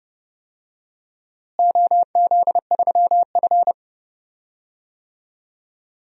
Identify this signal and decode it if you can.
Morse code, heard as a keyed tone, about 29 wpm, 700 Hz. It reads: OZ3F
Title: DXCC-100 Call Signs - 4 Characters - Repeated 3x - 30wpm